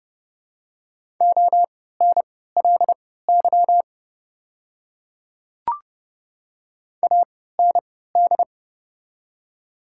Morse code, heard as a keyed tone, about 30 wpm, spaced as for 17 wpm, 690 Hz. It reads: ONLY AND
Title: 3x Character Spacing - Top 200 Words - 30wpm